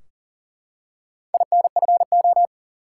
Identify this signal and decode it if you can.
Morse code, heard as a keyed tone, about 39 wpm, 700 Hz. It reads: INFO
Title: Top 180 QSO Elements 40wpm